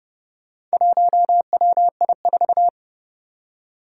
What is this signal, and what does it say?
Morse code, heard as a keyed tone, about 30 wpm, 700 Hz. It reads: 1WI4
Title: Four Challenging Letter-Number Combinations 30wpm